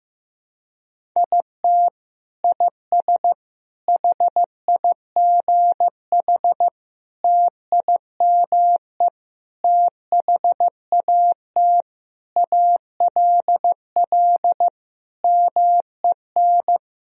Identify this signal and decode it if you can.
Morse code, heard as a keyed tone, about 16 wpm, 700 Hz. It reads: IT IS HIGH TIME THAT ALL MEN
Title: Sentences from Top 1000 Words - Review Part 1 - 15wpm